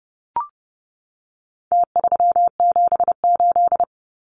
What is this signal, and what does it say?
Morse code, heard as a keyed tone, about 30 wpm, 700 Hz. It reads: T378
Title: Numbers - Contest Style - 4-Digit 30wpm